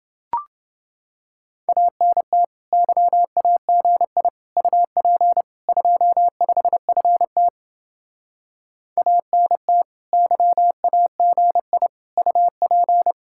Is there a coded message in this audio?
Yes: Morse code from 1.7 to 13.1 s, heard as a keyed tone, about 30 wpm, 700 Hz. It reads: ANT YAGI UP 25FT ANT YAGI UP